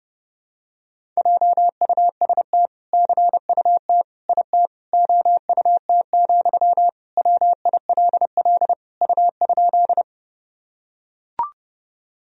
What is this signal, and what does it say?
Morse code, heard as a keyed tone, about 30 wpm, 700 Hz. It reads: JUST CUT IT OUT, WILL U?
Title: Sentences from Top 300 Words and Abbreviations 30wpm